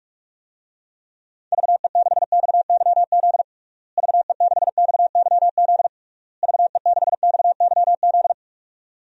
Morse code, heard as a keyed tone, about 41 wpm, 700 Hz. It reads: VE6XYZ VE6XYZ VE6XYZ
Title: DXCC-100 Call Signs - 6 Characters - Repeated 3x - 45wpm